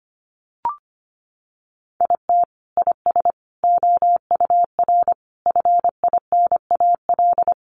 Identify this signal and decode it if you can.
Morse code, heard as a keyed tone, about 25 wpm, 700 Hz. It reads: IT IS OUR FINAL
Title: Sentences from Top 1200 Words 25wpm